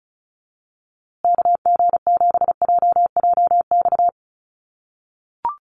Morse code, heard as a keyed tone, about 34 wpm, 700 Hz. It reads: KG7JJX
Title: USA Call Signs Repeated 2x 35wpm